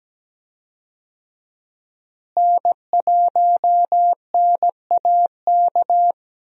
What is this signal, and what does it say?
Morse code, heard as a keyed tone, about 17 wpm, 700 Hz. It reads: N1NAK
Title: USA Call Signs - 5 Characters - Repeated 2x - 17wpm